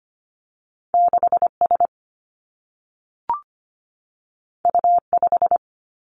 Morse code, heard as a keyed tone, about 25 wpm, 700 Hz. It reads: 6S U5
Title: Two Challenging Letter-Number Combinations 25wpm